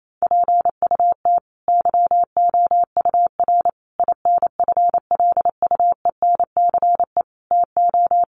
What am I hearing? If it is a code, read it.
Morse code, heard as a keyed tone, about 28 wpm, 700 Hz. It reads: PUT YOUR INFLUENCE TO
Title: Sentences from Top 1100 Words 28wpm